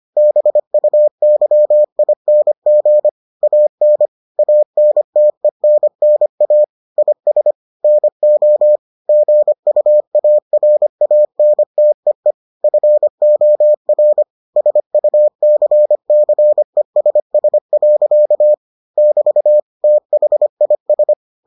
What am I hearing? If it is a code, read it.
Morse code, heard as a keyed tone, about 25 wpm, 600 Hz. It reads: BUYING AN ANTENNA IS NO GUARANTEE FOR SUCCESS. <BT> THIS